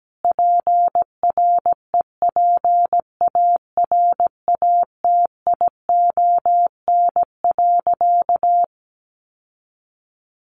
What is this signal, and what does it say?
Morse code, heard as a keyed tone, about 17 wpm, 700 Hz. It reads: PREPARATION.
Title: Sentences from Top 1400 Words 17wpm